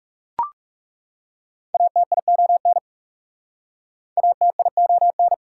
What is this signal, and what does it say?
Morse code, heard as a keyed tone, about 39 wpm, 690 Hz. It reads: ATION ATION